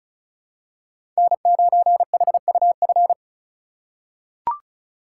Morse code, heard as a keyed tone, about 36 wpm, 700 Hz. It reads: N9HUF
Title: USA Call Signs - 5 Characters - 2x Character Spacing - 35wpm